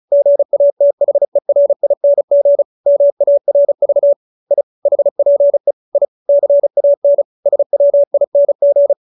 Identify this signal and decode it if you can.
Morse code, heard as a keyed tone, about 35 wpm, 570 Hz. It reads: GATHERING MARV I HPE I CAN SWING
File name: QSO312 35